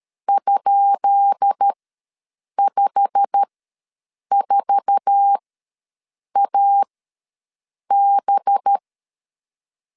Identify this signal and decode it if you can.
Morse code, heard as a keyed tone, about 13 wpm, spaced as for 7 wpm, 800 Hz. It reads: ?54AB
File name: LEZIONI / 10 SIGNS / 10B INT ws 12wpm 3527